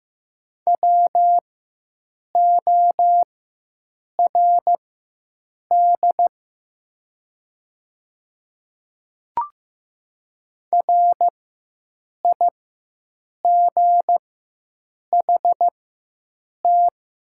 Morse code, heard as a keyed tone, about 15 wpm, spaced as for 7 wpm, 700 Hz. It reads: WORD RIGHT
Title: 4x Character Spacing - Top 500 Words - 15wpm